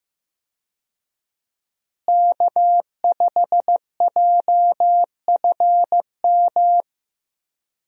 Morse code, heard as a keyed tone, about 15 wpm, 700 Hz. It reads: K5JFM
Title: USA Call Signs - 5 Characters - Repeated 3x - 15wpm